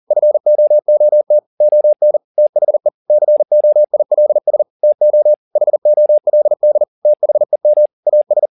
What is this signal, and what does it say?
Morse code, heard as a keyed tone, about 40 wpm, 600 Hz. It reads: FOOT ON THE COILS TO HOLD THEM AS